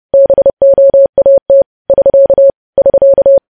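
Morse code, heard as a keyed tone, about 30 wpm, 550 Hz. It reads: BOAT <SK> <SK>
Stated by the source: News headlines sent in Morse code at 30.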